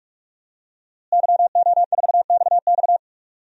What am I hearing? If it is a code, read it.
Morse code, heard as a keyed tone, about 46 wpm, 700 Hz. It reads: YO4XX
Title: DXCC-100 Call Signs - 5 Characters - Repeated 3x - 45wpm